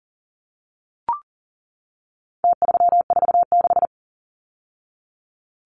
Morse code, heard as a keyed tone, about 40 wpm, 700 Hz. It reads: T346
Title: Numbers - Contest Style - 4-Digit 40wpm